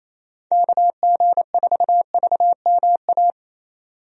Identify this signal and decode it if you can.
Morse code, heard as a keyed tone, about 27 wpm, 700 Hz. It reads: KG4VMA
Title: USA Call Signs - 6 Characters - 28wpm